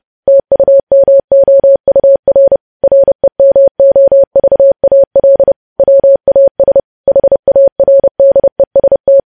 Morse code, heard as a keyed tone, about 30 wpm, 550 Hz. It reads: TUMOUR REMOVAL WAS HARDEST
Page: news headlines in morse code at 30 wpm